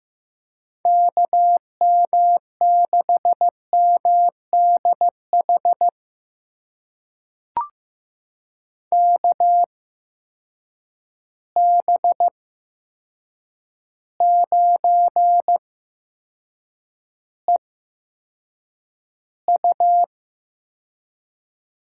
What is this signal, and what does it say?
Morse code, heard as a keyed tone, about 15 wpm, spaced as for 7 wpm, 700 Hz. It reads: KM6MDH KB9EU
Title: USA Call Signs - 8x Character Spacing - 15wpm